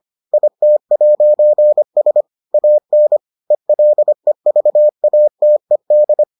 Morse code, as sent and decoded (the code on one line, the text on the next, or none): .. - .----. ... / .- -. / . .-.. ...- .- - . -..
IT'S AN ELEVATED